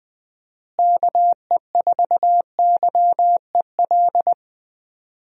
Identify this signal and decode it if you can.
Morse code, heard as a keyed tone, about 21 wpm, 700 Hz. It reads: KE4YEL
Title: USA Call Signs - 6 Characters - Repeated 3x - 20wpm